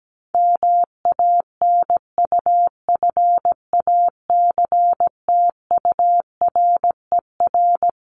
Morse code, heard as a keyed tone, about 17 wpm, 700 Hz. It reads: MANUFACTURER
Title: Top 1751-2000 Words 17wpm